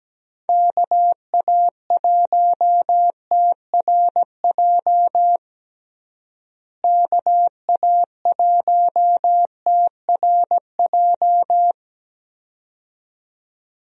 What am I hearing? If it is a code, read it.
Morse code, heard as a keyed tone, about 17 wpm, 700 Hz. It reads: KA1TRJ KA1TRJ
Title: USA Call Signs - 6 Characters - Repeated 2x - 17wpm